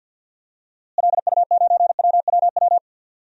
Morse code, heard as a keyed tone, about 47 wpm, 700 Hz. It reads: LU9WWW
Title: DXCC-100 Call Signs - 6 Characters - Repeated 2x - 50wpm